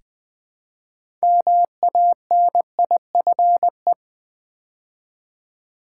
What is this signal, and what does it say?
Morse code, heard as a keyed tone, about 20 wpm, 700 Hz. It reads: MANIFE
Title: Build A Word - 3 Letters At A Time - 8-Letter Words - 20wpm